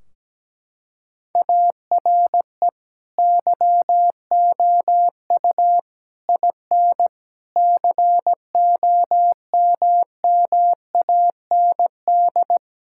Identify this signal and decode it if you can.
Morse code, heard as a keyed tone, about 17 wpm, 700 Hz. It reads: ARE YOU IN COMMAND
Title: Sentences from Top 1600 Words 17wpm